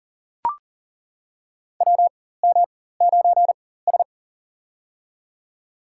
Morse code, heard as a keyed tone, about 40 wpm, spaced as for 19 wpm, 700 Hz. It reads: WM9S
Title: USA Call Signs - 4x Character Spacing - 40wpm